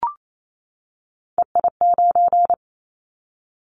Morse code, heard as a keyed tone, about 28 wpm, 700 Hz. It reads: EI9
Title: Callsign Prefixes for Top 50 DXCC - 28wpm